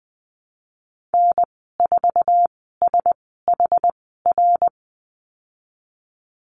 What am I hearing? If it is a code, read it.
Morse code, heard as a keyed tone, about 19 wpm, spaced as for 14 wpm, 700 Hz. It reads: N4SHR
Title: 076 Lesson 13g - Letter L - Call Signs 20wpm